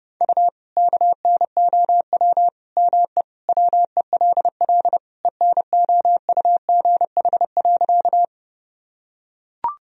Morse code, heard as a keyed tone, about 32 wpm, 700 Hz. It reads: U KNOW ME WELL ENOUGH.
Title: Sentences from Top 400 Words and Abbreviations 30wpm